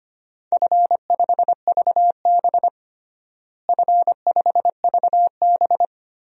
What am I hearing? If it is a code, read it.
Morse code, heard as a keyed tone, about 25 wpm, 700 Hz. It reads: F5VB F5VB
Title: DXCC-100 Call Signs - 4 Characters - Repeated 3x - 25wpm